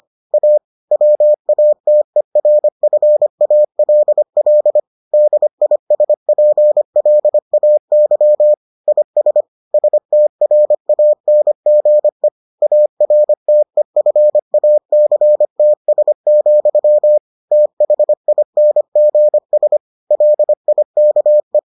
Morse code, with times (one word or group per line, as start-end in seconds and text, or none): 0.3-0.6 A
0.9-4.8 WATERFALL
5.1-8.5 DISPLAY
8.9-9.4 IS
9.7-12.3 STRANGE
12.6-17.2 ARTEFACTS,
17.5-19.8 THINGS
20.1-21.6 LIKE